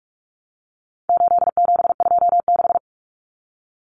Morse code, heard as a keyed tone, about 44 wpm, 700 Hz. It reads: 8726
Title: Common Cut Numbers - 4-Digit 45wpm